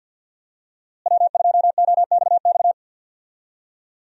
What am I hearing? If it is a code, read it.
Morse code, heard as a keyed tone, about 52 wpm, 700 Hz. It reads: W2OXX
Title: USA Call Signs - 5 Characters - Repeated 2x - 50wpm